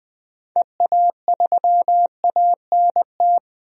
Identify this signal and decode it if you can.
Morse code, heard as a keyed tone, about 21 wpm, 700 Hz. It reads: EA3ANT
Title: DXCC-100 Call Signs - 6 Characters - 20wpm